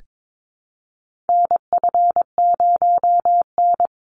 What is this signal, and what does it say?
Morse code, heard as a keyed tone, about 22 wpm, 700 Hz. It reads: NF0N
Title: Call Signs Repeated 3x - 22wpm